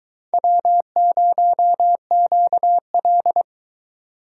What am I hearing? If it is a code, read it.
Morse code, heard as a keyed tone, about 22 wpm, 700 Hz. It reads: W0QL